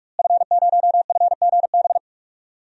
Morse code, heard as a keyed tone, about 41 wpm, 700 Hz. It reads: F0FGB
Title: DXCC-100 Call Signs - 5 Characters - Repeated 3x - 45wpm